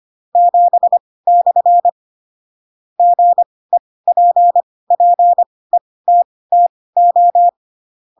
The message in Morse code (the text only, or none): 7/ GEPPETTO